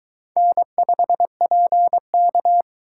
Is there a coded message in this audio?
Yes: Morse code from 0.4 to 2.6 s, heard as a keyed tone, about 22 wpm, 700 Hz. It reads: N5PK